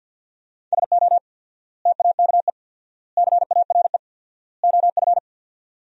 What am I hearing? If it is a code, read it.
Morse code, heard as a keyed tone, about 53 wpm, 700 Hz. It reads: SO TAKE CARE OF
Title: 4x Word-Spacing - Sentences from Top 300 Words - 50wpm